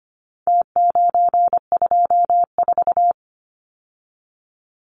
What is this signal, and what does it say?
Morse code, heard as a keyed tone, about 25 wpm, 700 Hz. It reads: T924
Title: Numbers - Contest Style - 4-Digit 25wpm